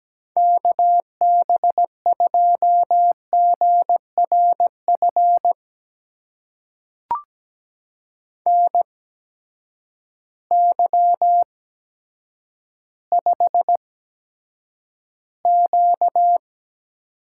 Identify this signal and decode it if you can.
Morse code, heard as a keyed tone, about 17 wpm, spaced as for 9 wpm, 700 Hz. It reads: KB2GRF NY5Q
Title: USA Call Signs - 8x Character Spacing - 17wpm